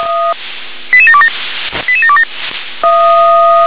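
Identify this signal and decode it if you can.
Five-tone selective call (ZVEI), heard as quick runs of five tones, tones of about 70 ms. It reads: 88727 88727
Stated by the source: ZVEI five-tone selective call received on an SDR.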